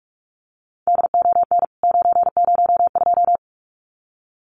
Morse code, heard as a keyed tone, about 44 wpm, 700 Hz. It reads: DON 902
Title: CWT Exchange 45wpm